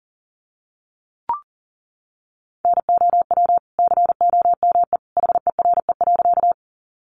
Morse code, heard as a keyed tone, about 40 wpm, 700 Hz. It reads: NOW COME HERE.
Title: Sentences from Top 200 Words 40wpm